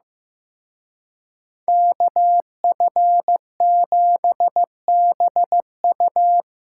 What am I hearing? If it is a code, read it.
Morse code, heard as a keyed tone, about 18 wpm, 700 Hz. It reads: KF7BU
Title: Speed Racing - USA 2x2 Call Signs 15wpm